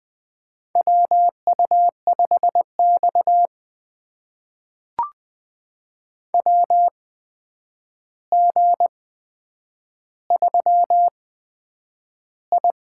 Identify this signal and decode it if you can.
Morse code, heard as a keyed tone, about 20 wpm, spaced as for 9 wpm, 700 Hz. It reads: WU5X WG3I
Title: USA Call Signs - 4 Characters - 8x Character Spacing - 20wpm